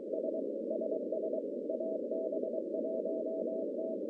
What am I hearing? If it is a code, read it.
Morse code, heard as a keyed tone, about 23 wpm, 640 Hz, with noise. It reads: S SS AB1T
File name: contests / CW-SS-k5tr- -1100